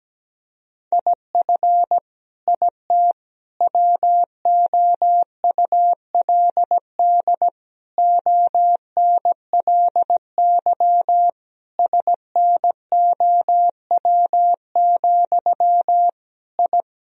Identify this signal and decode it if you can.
Morse code, heard as a keyed tone, about 17 wpm, 700 Hz. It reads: IF IT WOULD ONLY SNOW, I
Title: Sentences from Top 1000 Words - Review Part 3 - 17wpm